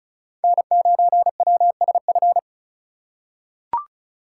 Morse code, heard as a keyed tone, about 34 wpm, 700 Hz. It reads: N9WSF
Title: USA Call Signs - 5 Characters - 4x Character Spacing - 35wpm